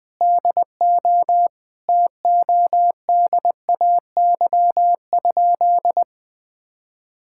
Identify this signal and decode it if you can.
Morse code, heard as a keyed tone, about 20 wpm, 700 Hz. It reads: DO TODAY?
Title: Sentences with 4-Word Phrases - 20wpm